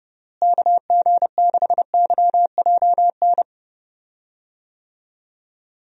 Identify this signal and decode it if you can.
Morse code, heard as a keyed tone, about 32 wpm, 700 Hz. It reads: KG6YJN
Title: USA Call Signs - 6 Characters - Repeated 2x - 30wpm